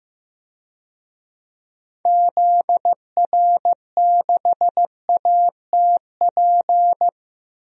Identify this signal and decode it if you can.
Morse code, heard as a keyed tone, about 15 wpm, 700 Hz. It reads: ZR6ATP
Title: DXCC-100 Call Signs - 6 Characters - 15wpm